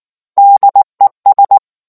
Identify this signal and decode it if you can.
Morse code, heard as a keyed tone, about 19 wpm, 800 Hz. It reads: DES